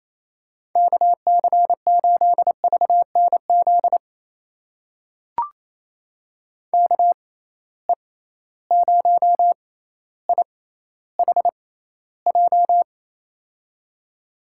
Morse code, heard as a keyed tone, about 28 wpm, spaced as for 14 wpm, 700 Hz. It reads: KC8VNZ KE0IHJ
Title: USA Call Signs - 6 Characters - 6x Character Spacing - 28wpm